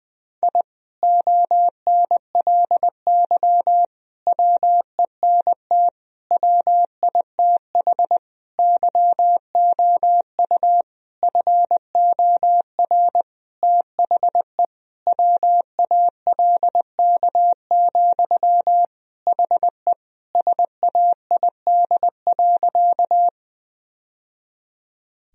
Morse code, heard as a keyed tone, about 20 wpm, 700 Hz. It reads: I ONLY WENT WITH YOU FOR THE WALK, HE SAID.
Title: Sentences from Top 1000 Words - Review Part 1 - 20wpm